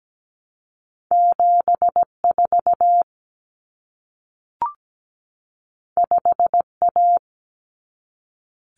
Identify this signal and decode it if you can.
Morse code, heard as a keyed tone, about 17 wpm, 700 Hz. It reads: 74 5A
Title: Common Cut Numbers - 2-Digit 17wpm